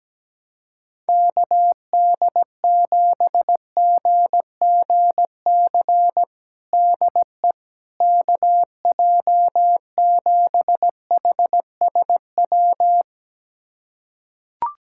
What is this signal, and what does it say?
Morse code, heard as a keyed tone, about 17 wpm, 700 Hz. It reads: KD7GGC DE KJ7HSW
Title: USA Callsign DE Callsign - 6 Characters - 17wpm